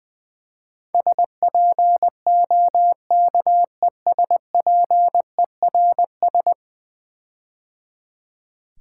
Morse code, heard as a keyed tone, about 20 wpm, 700 Hz. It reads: SPOKESPERS
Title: Build A Word - 5 Letters At A Time - 12-Letter Words - 20wpm